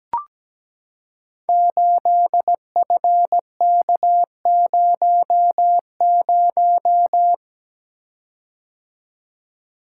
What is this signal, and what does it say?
Morse code, heard as a keyed tone, about 17 wpm, 700 Hz. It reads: 8FK00
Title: Code Groups - 5 Letter:Numbers - 17wpm